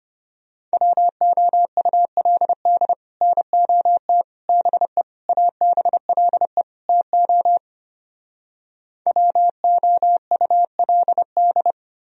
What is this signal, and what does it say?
Morse code, heard as a keyed tone, about 33 wpm, 700 Hz. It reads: WOULD NOT BE ABLE TO WOULD